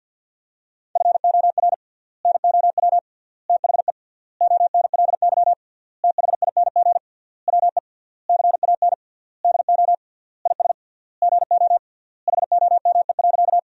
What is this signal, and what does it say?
Morse code, heard as a keyed tone, about 50 wpm, 700 Hz. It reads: FOR NOW THE ONLY THING WE CAN DO IS GO HOME.
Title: 3x Word-Spacing - Sentences from Top 200 Words - 50wpm